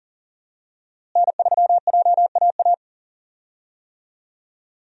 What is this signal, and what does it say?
Morse code, heard as a keyed tone, about 40 wpm, 700 Hz. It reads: N3JAA
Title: USA Call Signs - 5 Characters - Repeated 2x - 40wpm